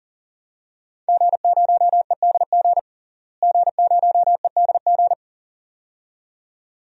Morse code, heard as a keyed tone, about 40 wpm, 700 Hz. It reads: G0EDG G0EDG
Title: DXCC-100 Call Signs - 5 Characters - Repeated 2x - 40wpm